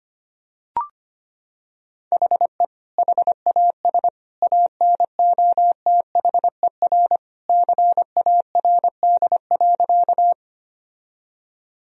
Morse code, heard as a keyed tone, about 25 wpm, 700 Hz. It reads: HE HAS ANOTHER CARD.
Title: Sentences from Top 1000 Words - Review Part 4 - 25wpm